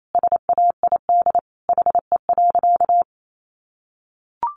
Morse code, heard as a keyed tone, about 28 wpm, 700 Hz. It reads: SAID HE.
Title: Sentences from Top 100 Words 28wpm